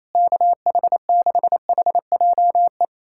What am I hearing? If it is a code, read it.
Morse code, heard as a keyed tone, about 28 wpm, 700 Hz. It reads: KH6HJE
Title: USA Callsign DE Callsign 28wpm